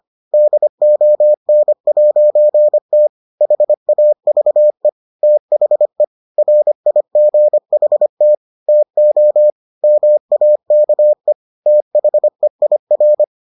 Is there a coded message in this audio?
Yes: Morse code from 0.3 to 13.2 s, heard as a keyed tone, about 25 wpm, 600 Hz. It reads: DON'T HAVE THE RIGHT TO MAKE THEIR